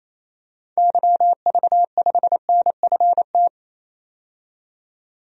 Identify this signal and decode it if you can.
Morse code, heard as a keyed tone, about 28 wpm, 700 Hz. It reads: YV5NFT
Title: DXCC-100 Call Signs - 6 Characters - 28wpm